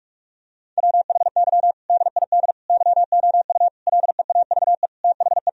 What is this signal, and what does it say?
Morse code, heard as a keyed tone, about 41 wpm, 700 Hz. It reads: WHY DID YOU LEAVE THE
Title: Sentences from Top 1000 Words - Review Part 3 - 45wpm